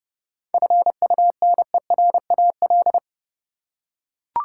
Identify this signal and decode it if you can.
Morse code, heard as a keyed tone, about 30 wpm, 700 Hz. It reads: FUNERAL
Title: Seven Letter Words - 30wpm